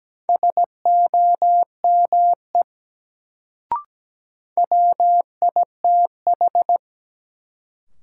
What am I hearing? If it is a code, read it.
Morse code, heard as a keyed tone, about 17 wpm, 700 Hz. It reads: SOME WITH
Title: Sentences from Top 100 Words - New Word Drill - 17wpm